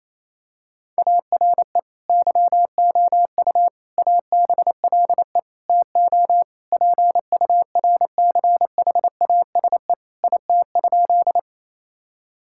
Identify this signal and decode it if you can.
Morse code, heard as a keyed tone, about 28 wpm, 700 Hz. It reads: ARE YOU ABLE TO PURCHASE IT?
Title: Sentences from Top 1000 Words - Review Part 5 - 28wpm